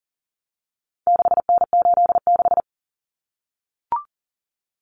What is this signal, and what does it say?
Morse code, heard as a keyed tone, about 40 wpm, 700 Hz. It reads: 6N86
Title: Common Cut Numbers - 4-Digit 40wpm